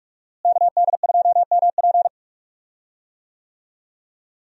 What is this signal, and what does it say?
Morse code, heard as a keyed tone, about 46 wpm, 700 Hz. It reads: KD2MP
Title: USA Call Signs - 5 Characters - Repeated 2x - 45wpm